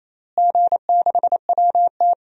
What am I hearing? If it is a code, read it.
Morse code, heard as a keyed tone, about 28 wpm, 700 Hz. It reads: G6WT
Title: DXCC-100 Call Signs - 4 Characters - Repeated 2x - 28wpm